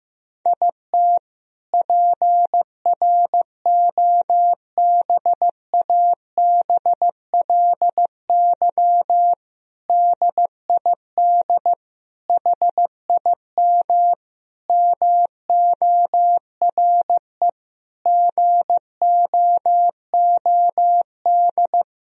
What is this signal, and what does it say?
Morse code, heard as a keyed tone, about 15 wpm, 700 Hz. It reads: IT PROBABLY DID HIM MORE GOOD
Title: Sentences from Top 1000 Words - Review Part 4 - 15wpm